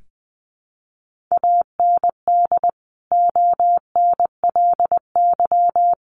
Morse code, heard as a keyed tone, about 20 wpm, 700 Hz. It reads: AND ONLY
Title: Sentences from Top 1400 Words 20wpm